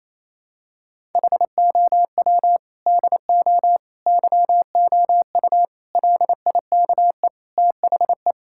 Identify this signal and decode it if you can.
Morse code, heard as a keyed tone, about 28 wpm, 700 Hz. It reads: HOW DO YOU LIKE THE
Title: Sentences from Top 1000 Words - Review Part 6 - 28wpm